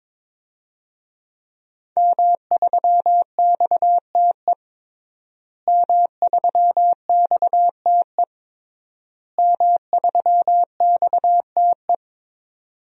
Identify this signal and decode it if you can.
Morse code, heard as a keyed tone, about 23 wpm, 700 Hz. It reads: M3XTE M3XTE M3XTE
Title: DXCC-100 Call Signs - 5 Characters - Repeated 3x - 22wpm